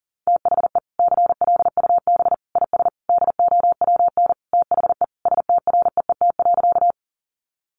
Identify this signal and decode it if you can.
Morse code, heard as a keyed tone, about 40 wpm, 700 Hz. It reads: THE CLUB IS DOWN THE STREET.
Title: Sentences from Top 1200 Words 40wpm